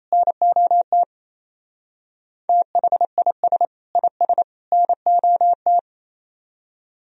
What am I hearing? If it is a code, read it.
Morse code, heard as a keyed tone, about 35 wpm, 700 Hz. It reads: NOT THIS IS NOT